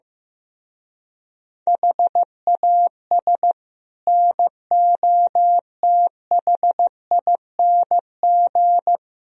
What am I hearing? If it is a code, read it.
Morse code, heard as a keyed tone, about 15 wpm, 700 Hz. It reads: HAS NOTHING
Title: Speed-Racing - Sets of 5 Words 15wpm